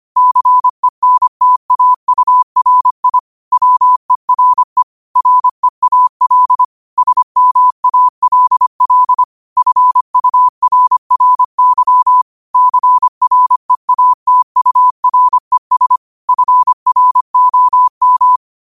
Morse code, heard as a keyed tone, about 25 wpm, 1000 Hz. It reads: CENTAURI WERE REAL SMALL FURRY CREATURES FROM
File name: qotdfiles / 25wpm